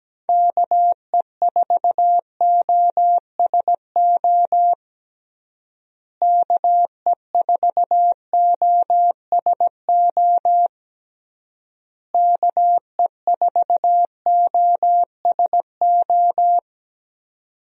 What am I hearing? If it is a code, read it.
Morse code, heard as a keyed tone, about 17 wpm, 700 Hz. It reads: KE4OSO KE4OSO KE4OSO
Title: USA Call Signs - 6 Characters - Repeated 3x - 17wpm